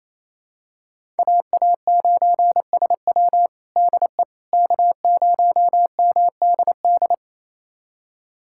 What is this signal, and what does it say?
Morse code, heard as a keyed tone, about 28 wpm, 700 Hz. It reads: AA9SW DE K0MDD
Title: USA Callsign DE Callsign 28wpm